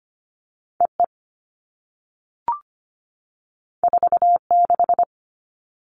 Morse code, heard as a keyed tone, about 25 wpm, 700 Hz. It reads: EE 46
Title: All Cut Numbers - 2-Digit 25wpm